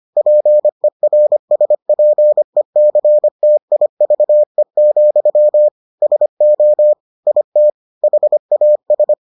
Morse code, heard as a keyed tone, about 25 wpm, 600 Hz. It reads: PERSPECTIVE, SO IT HAS